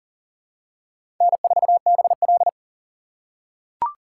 Morse code, heard as a keyed tone, about 40 wpm, 700 Hz. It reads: N4BL
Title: USA Call Signs - 4 Characters - 6x Character Spacing - 40wpm